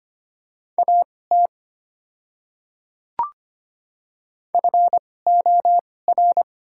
Morse code, heard as a keyed tone, about 25 wpm, spaced as for 18 wpm, 690 Hz. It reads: AT FOR
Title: 2x Character Spacing - Top 500 Words - 25wpm